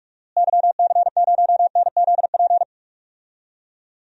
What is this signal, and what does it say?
Morse code, heard as a keyed tone, about 44 wpm, 700 Hz. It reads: YC0NZP
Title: DXCC-100 Call Signs - 6 Characters - Repeated 2x - 45wpm